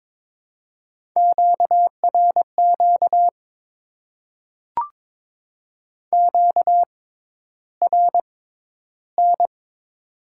Morse code, heard as a keyed tone, about 22 wpm, spaced as for 11 wpm, 700 Hz. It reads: QRQ QRN
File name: Top 90 QSO Elements - 6x Character Spacing - 22wpm